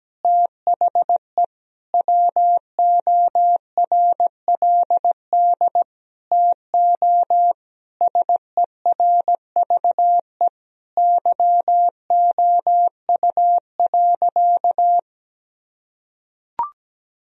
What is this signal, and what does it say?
Morse code, heard as a keyed tone, about 17 wpm, 700 Hz. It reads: THE WORLD TO SERVE YOU.
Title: Sentences from Top 1000 Words - Review Part 2 - 17wpm